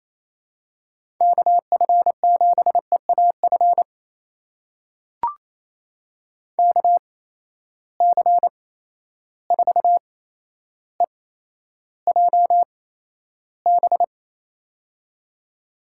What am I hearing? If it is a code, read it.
Morse code, heard as a keyed tone, about 28 wpm, spaced as for 12 wpm, 700 Hz. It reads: KF7EAF KC4EJB
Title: USA Call Signs - 6 Characters - 8x Character Spacing - 28wpm